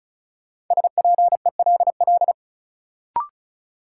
Morse code, heard as a keyed tone, about 36 wpm, 700 Hz. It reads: SPELL
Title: Sentences from Top 1000 Words - Review Part 1 - New Word Drill - 35wpm